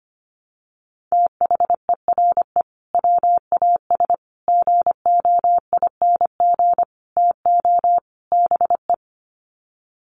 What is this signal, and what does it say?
Morse code, heard as a keyed tone, about 25 wpm, 700 Hz. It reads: THERE WAS GOING TO BE
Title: Sets of 5 Words Encore2 25wpm